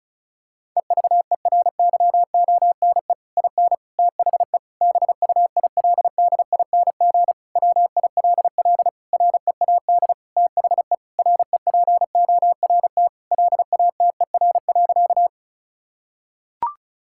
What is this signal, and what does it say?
Morse code, heard as a keyed tone, about 36 wpm, 700 Hz. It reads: EVERYONE IN THE BUILDING WILL READ THE REPORT LATER.
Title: Sentences from Top 1000 Words - Review Part 6 - 35wpm